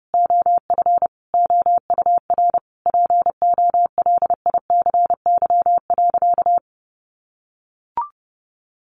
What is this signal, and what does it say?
Morse code, heard as a keyed tone, about 30 wpm, 700 Hz. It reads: OF OUR POLICY.
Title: Sentences from Top 1200 Words 30wpm